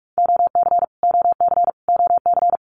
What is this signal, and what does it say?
Morse code, heard as a keyed tone, about 46 wpm, 700 Hz. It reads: OC OC OC
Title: Abbreviations Repeated 3x - 45wpm